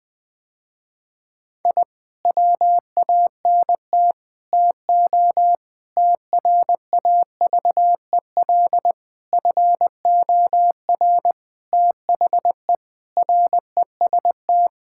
Morse code, heard as a keyed tone, about 20 wpm, 700 Hz. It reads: I WANT TO TRAVEL FOR THE REST
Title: Sentences with 5-Word Phrases - Spoken Beginning - 20wpm